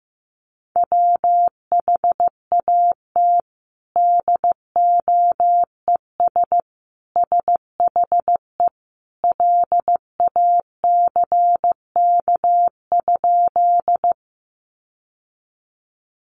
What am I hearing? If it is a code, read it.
Morse code, heard as a keyed tone, about 15 wpm, 700 Hz. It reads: WHAT DOES SHE LACK?
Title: Sentences from Top 1100 Words 15wpm